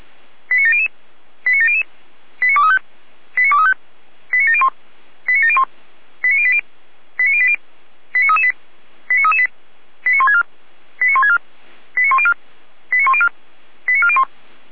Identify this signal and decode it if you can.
Five-tone selective call (ZVEI), heard as quick runs of five tones, tones of about 70 ms. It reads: 89700 89700 89235 89235 89701 89701 89080 89080 89338 89338 89164 89164 89104 89104 89401